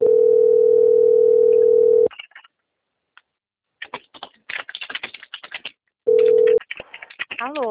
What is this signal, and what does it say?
Ringback tone (North America), 440 + 480 Hz.